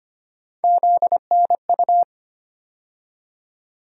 Morse code, heard as a keyed tone, about 25 wpm, 700 Hz. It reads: ZNU
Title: USA Call Sign Postfixes - 3 Letters 25wpm